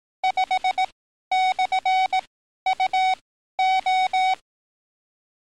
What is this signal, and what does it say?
Morse code, heard as a keyed tone, about 18 wpm, spaced as for 12 wpm, 750 Hz. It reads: 5/UO
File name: training / NAVPERS 11049to11060 / NAVPERS 11053 RA RB